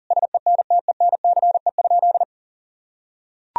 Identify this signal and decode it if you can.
Morse code, heard as a keyed tone, about 40 wpm, 700 Hz. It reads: SENTENCE?
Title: Sentences from Top 1000 Words - Review Part 1 - 40wpm